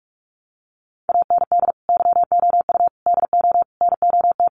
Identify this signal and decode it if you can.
Morse code, heard as a keyed tone, about 44 wpm, 700 Hz. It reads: AND YOU DO NOT
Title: Sets of 5 Words Encore 45wpm